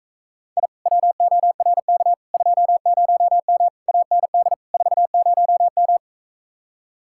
Morse code, heard as a keyed tone, about 45 wpm, 700 Hz. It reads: I WORK 20M AND 40M